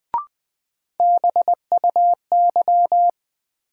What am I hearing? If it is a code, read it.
Morse code, heard as a keyed tone, about 21 wpm, 700 Hz. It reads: BUY
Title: Sentences from Top 700 Words - New Word Drill - 20wpm